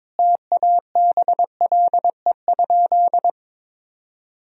Morse code, heard as a keyed tone, about 21 wpm, 700 Hz. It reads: TABLE?
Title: Sentences from Top 1000 Words - Review Part 4 - 22wpm